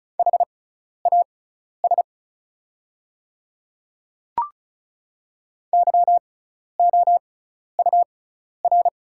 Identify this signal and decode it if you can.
Morse code, heard as a keyed tone, about 35 wpm, spaced as for 12 wpm, 690 Hz. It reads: HAS YOUR
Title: 6x Character Spacing - Top 100 Words - 35wpm